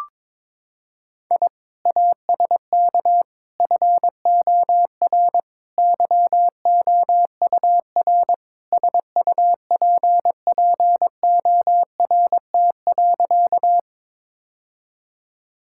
Morse code, heard as a keyed tone, about 22 wpm, 700 Hz. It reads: I ASK FOR YOUR SUPPORT.
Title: Sentences from Top 1000 Words - Review Part 5 - 22wpm